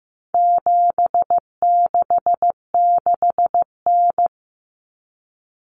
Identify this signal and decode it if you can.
Morse code, heard as a keyed tone, about 15 wpm, 700 Hz. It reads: 766N
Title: Common Cut Numbers - 4-Digit 15wpm